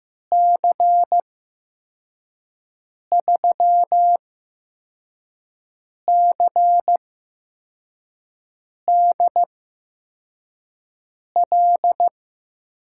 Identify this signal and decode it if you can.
Morse code, heard as a keyed tone, about 15 wpm, spaced as for 4 wpm, 700 Hz. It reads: C3CDL
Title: USA Call Signs - 6 Characters - 8x Character Spacing - 15wpm